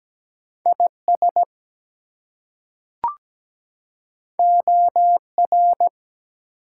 Morse code, heard as a keyed tone, about 17 wpm, 700 Hz. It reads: IS OR
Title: Two Letter Words - 17wpm